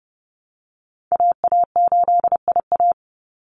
Morse code, heard as a keyed tone, about 30 wpm, 700 Hz. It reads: AA8IA
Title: Call Signs Repeated 3x - 30wpm